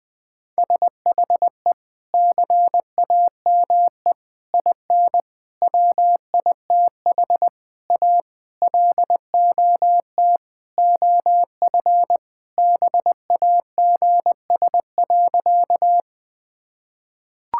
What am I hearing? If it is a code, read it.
Morse code, heard as a keyed tone, about 20 wpm, 700 Hz. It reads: SHE CAME IN WITH A LOT OF BAGS.
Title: Sentences with 4-Word Phrases - 20wpm